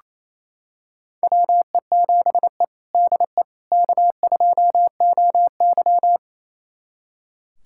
Morse code, heard as a keyed tone, about 28 wpm, 700 Hz. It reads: WE7E DE K2OY
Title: USA Callsign DE Callsign - 4 Characters - 28wpm